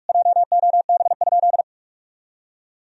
Morse code, heard as a keyed tone, about 46 wpm, 700 Hz. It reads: JOB?
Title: Sentences from Top 1000 Words - Review Part 3 - 45wpm